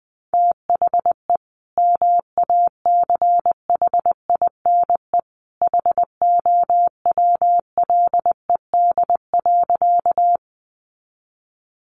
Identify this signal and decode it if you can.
Morse code, heard as a keyed tone, about 20 wpm, 700 Hz. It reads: THE MACHINE HOWLED.
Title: A Sound Of Thunder 20wpm